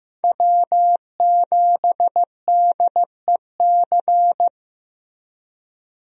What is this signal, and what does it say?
Morse code, heard as a keyed tone, about 15 wpm, 700 Hz. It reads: W7DEC
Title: USA Call Signs - 5 Characters - Repeated 2x - 15wpm